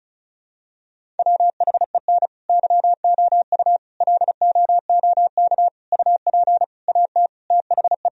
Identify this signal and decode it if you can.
Morse code, heard as a keyed tone, about 34 wpm, 700 Hz. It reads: WHEN YOU LOOK UP AT THE
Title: Sentences from Top 1000 Words - Review Part 1 - 35wpm